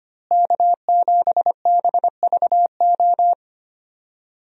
Morse code, heard as a keyed tone, about 25 wpm, 700 Hz. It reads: K7BVO
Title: USA Call Signs - 5 Characters - Repeated 2x - 25wpm